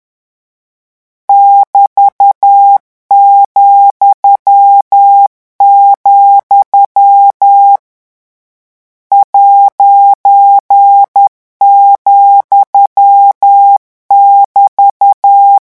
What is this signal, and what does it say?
Morse code, heard as a keyed tone, about 10 wpm, 800 Hz. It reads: <BT>,, ',<BT>
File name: morse mp3 / lecon14-vitesse 10